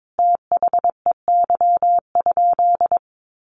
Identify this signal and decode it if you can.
Morse code, heard as a keyed tone, about 22 wpm, 700 Hz. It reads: THEY?
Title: Sentences from Top 1200 Words 22wpm